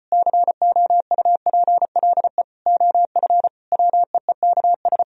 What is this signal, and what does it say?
Morse code, heard as a keyed tone, about 33 wpm, 700 Hz. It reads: COUPLE OF WEEKS